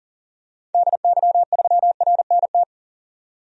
Morse code, heard as a keyed tone, about 40 wpm, 700 Hz. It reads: DY3RNT
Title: DXCC-100 Call Signs - 6 Characters - 40wpm